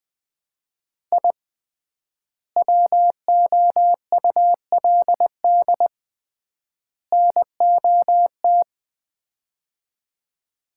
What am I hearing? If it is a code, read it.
Morse code, heard as a keyed tone, about 20 wpm, 700 Hz. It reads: I WOULD NOT
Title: 3x Word-Spacing - Sets of 3 Words - 20wpm